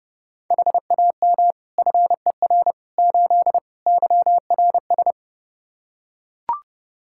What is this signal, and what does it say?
Morse code, heard as a keyed tone, about 32 wpm, 700 Hz. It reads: HAM FER 8 YRS
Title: Practice QSO - Exchange 3 - More Info - 30wpm